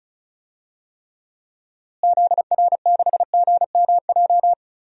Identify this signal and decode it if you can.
Morse code, heard as a keyed tone, about 36 wpm, 700 Hz. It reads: ZR6GMJ
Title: DXCC-100 Call Signs - 6 Characters - Repeated 3x - 35wpm